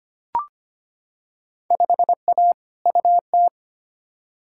Morse code, heard as a keyed tone, about 25 wpm, 700 Hz. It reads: 5A UT
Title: Field Day Exchanges - 25wpm